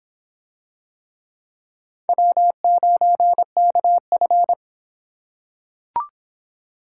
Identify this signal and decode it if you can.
Morse code, heard as a keyed tone, about 26 wpm, 700 Hz. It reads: W9KF